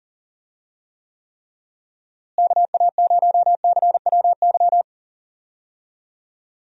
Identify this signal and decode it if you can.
Morse code, heard as a keyed tone, about 40 wpm, 700 Hz. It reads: KA0CWY
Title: USA Call Signs - 6 Characters - Repeated 3x - 40wpm